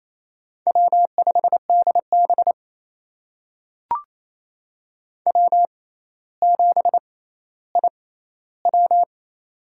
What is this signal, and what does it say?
Morse code, heard as a keyed tone, about 28 wpm, spaced as for 14 wpm, 700 Hz. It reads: W5DB W7IW
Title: USA Call Signs - 4 Characters - 6x Character Spacing - 28wpm